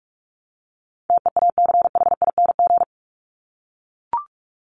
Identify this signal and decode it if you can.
Morse code, heard as a keyed tone, about 44 wpm, 700 Hz. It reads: TEACHING
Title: ING Suffix - Top 100 Words with Base Words 45wpm